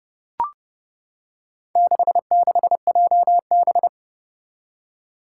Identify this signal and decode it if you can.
Morse code, heard as a keyed tone, about 30 wpm, 700 Hz. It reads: 66JB
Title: Four Challenging Letter-Number Combinations 30wpm